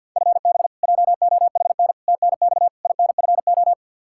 Morse code, heard as a keyed tone, about 47 wpm, 700 Hz. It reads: FB JOHN TNX INFO
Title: Practice QSO - Exchange 3 - More Info - 50wpm